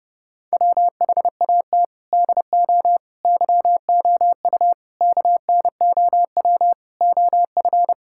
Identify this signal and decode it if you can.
Morse code, heard as a keyed tone, about 32 wpm, 700 Hz. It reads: WHAT DO YOU KNOW OF
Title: Sentences from Top 1000 Words - Review Part 5 - 30wpm